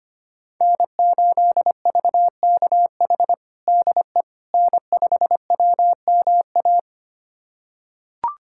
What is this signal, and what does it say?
Morse code, heard as a keyed tone, about 25 wpm, 700 Hz. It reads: N8VKH DE N5WMA
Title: USA Callsign DE Callsign - 5 Characters - 25wpm